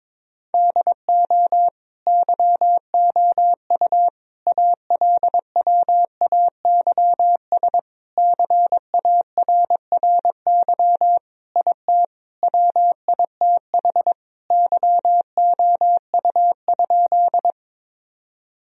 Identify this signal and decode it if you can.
Morse code, heard as a keyed tone, about 21 wpm, 700 Hz. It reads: DO YOU ALWAYS CARRY IT WITH YOU?
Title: Sentences from Top 1000 Words - Review Part 1 - 22wpm